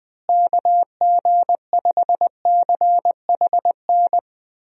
Morse code, heard as a keyed tone, about 20 wpm, 700 Hz. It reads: KG5CHN
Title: USA Call Signs - 6 Characters - Repeated 3x - 20wpm